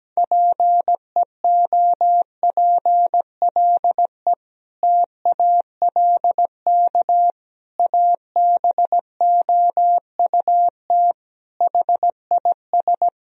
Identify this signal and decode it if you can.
Morse code, heard as a keyed tone, about 17 wpm, 700 Hz. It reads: PEOPLE TALK ABOUT HIS
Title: Sentences from Top 1000 Words - Review Part 3 - 17wpm